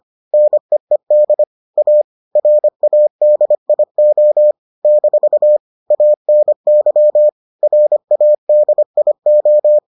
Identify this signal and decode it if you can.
Morse code, heard as a keyed tone, about 25 wpm, 600 Hz. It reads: NEED A RADIO - ANY RADIO